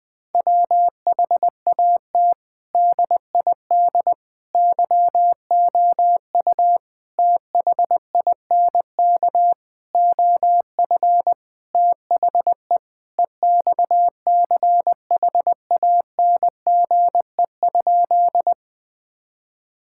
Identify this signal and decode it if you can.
Morse code, heard as a keyed tone, about 20 wpm, 700 Hz. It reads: WHAT DID YOU THINK OF THE EXCHANGE?
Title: Sentences from Top 1000 Words - Review Part 6 - 20wpm